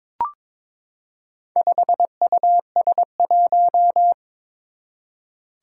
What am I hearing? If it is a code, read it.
Morse code, heard as a keyed tone, about 22 wpm, 700 Hz. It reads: 5US1
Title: Four Challenging Letter-Number Combinations 22wpm